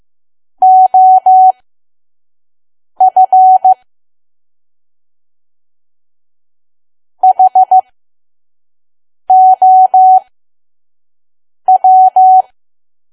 Morse code, heard as a keyed tone, about 15 wpm, spaced as for 5 wpm, 750 Hz. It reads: OF HOW
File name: Archive / 5 WPM files / PM